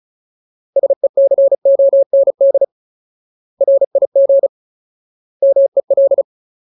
Morse code, heard as a keyed tone, about 34 wpm, 550 Hz. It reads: SECOND RIG MEL
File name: Ultimate Word List / 35WPM Ultimate